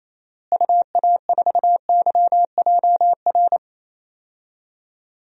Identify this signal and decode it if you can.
Morse code, heard as a keyed tone, about 28 wpm, 700 Hz. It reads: UA4YJR
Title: DXCC-100 Call Signs - 6 Characters - Repeated 3x - 28wpm